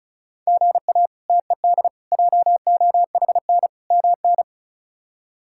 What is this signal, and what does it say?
Morse code, heard as a keyed tone, about 34 wpm, 700 Hz. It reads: GA TED JOHN MN
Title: K1USN SST QSO Practice - 35wpm